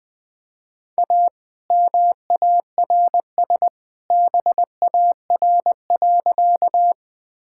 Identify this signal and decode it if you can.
Morse code, heard as a keyed tone, about 21 wpm, 700 Hz. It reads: A MARS BAR.
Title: All-Age Appropriate Jokes - 20wpm